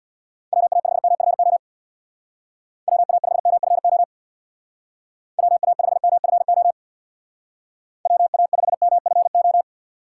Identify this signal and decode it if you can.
Morse code, heard as a keyed tone, about 64 wpm, 700 Hz. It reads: WA5MFO WA5MFO WA5MFO WA5MFO